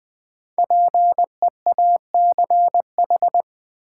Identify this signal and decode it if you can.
Morse code, heard as a keyed tone, about 20 wpm, 700 Hz. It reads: PEACH
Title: Difficult Words with Hints - 20wpm